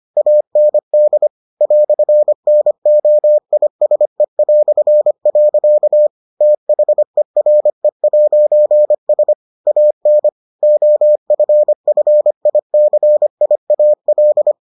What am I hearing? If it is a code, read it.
Morse code, heard as a keyed tone, about 25 wpm, 600 Hz. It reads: AND "NOISE". THERE'S AN OFFICIAL